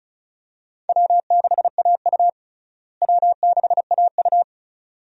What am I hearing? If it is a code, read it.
Morse code, heard as a keyed tone, about 34 wpm, 700 Hz. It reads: W6AU W6AU
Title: USA Call Signs - 4 Characters - Repeated 2x - 35wpm